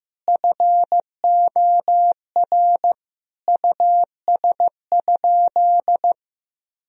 Morse code, heard as a keyed tone, about 15 wpm, 700 Hz. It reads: FOR US?
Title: Sentences from Top 1000 Words - Review Part 2 - 15wpm